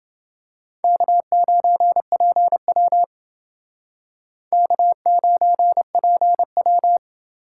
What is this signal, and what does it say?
Morse code, heard as a keyed tone, about 29 wpm, 700 Hz. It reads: K9PW K9PW